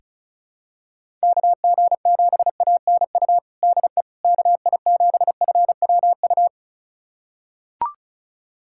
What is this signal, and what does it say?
Morse code, heard as a keyed tone, about 36 wpm, 700 Hz. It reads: KG7ANU DE KI7FWU
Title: USA Callsign DE Callsign - 6 Characters - 35wpm